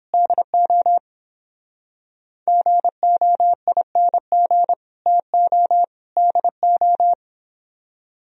Morse code, heard as a keyed tone, about 30 wpm, 700 Hz. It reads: DO GOING TO DO